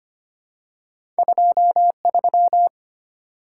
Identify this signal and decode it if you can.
Morse code, heard as a keyed tone, about 25 wpm, 700 Hz. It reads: 23
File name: Two Letter-Number Combinations - 25wpm